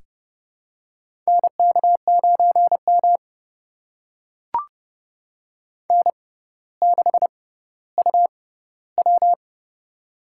Morse code, heard as a keyed tone, about 30 wpm, spaced as for 15 wpm, 700 Hz. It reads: NK9M N6UW
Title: USA Call Signs - 4 Characters - 6x Character Spacing - 30wpm